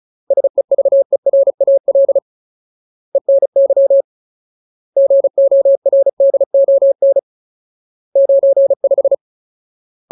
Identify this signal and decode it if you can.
Morse code, heard as a keyed tone, about 34 wpm, 550 Hz. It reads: SEVERAL ENY GORDON 95
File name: Ultimate Word List / 35WPM Ultimate